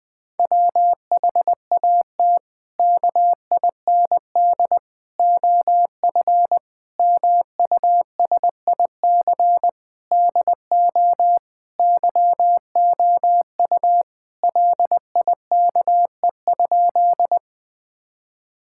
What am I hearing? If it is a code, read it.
Morse code, heard as a keyed tone, about 20 wpm, 700 Hz. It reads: WHAT KIND OF MUSIC DO YOU LIKE?
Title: Sentences with 3-Word Phrases - 20wpm